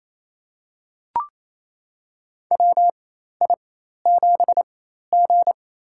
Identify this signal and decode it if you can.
Morse code, heard as a keyed tone, about 28 wpm, spaced as for 13 wpm, 700 Hz. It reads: WI7G
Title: USA Call Signs - 4 Characters - 4x Character Spacing - 28wpm